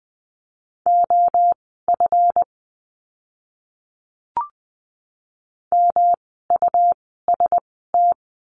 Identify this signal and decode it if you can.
Morse code, heard as a keyed tone, about 20 wpm, spaced as for 14 wpm, 700 Hz. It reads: OF MUST
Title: 237 Lesson 38f - Letter q - Words 20wpm